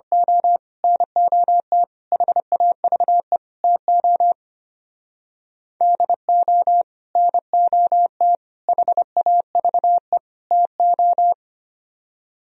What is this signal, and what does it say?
Morse code, heard as a keyed tone, about 27 wpm, 700 Hz. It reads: O NOT HAVE TO DO NOT HAVE TO
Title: Speed-Racing - Sets of 4 Words 25wpm